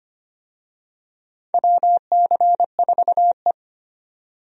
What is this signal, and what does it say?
Morse code, heard as a keyed tone, about 25 wpm, 700 Hz. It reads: WC4E
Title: USA Call Signs - 4 Characters - 25wpm